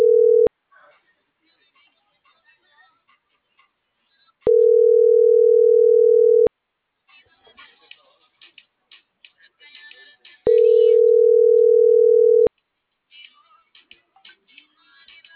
Ringback tone (North America), 440 + 480 Hz.